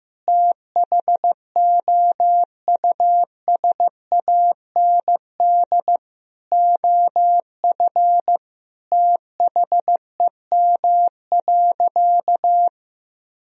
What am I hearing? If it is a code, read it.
Morse code, heard as a keyed tone, about 16 wpm, 700 Hz. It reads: THOUSAND OF THEM.
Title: Sentences from Top 1000 Words - Review Part 3 - 15wpm